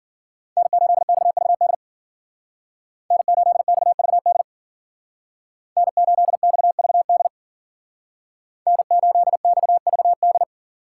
Morse code, heard as a keyed tone, about 50 wpm, 700 Hz. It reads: N8XVD N8XVD N8XVD N8XVD